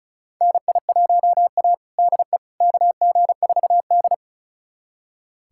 Morse code, heard as a keyed tone, about 36 wpm, 700 Hz. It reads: NI1A DE KG4D
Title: USA Callsign DE Callsign - 4 Characters - 35wpm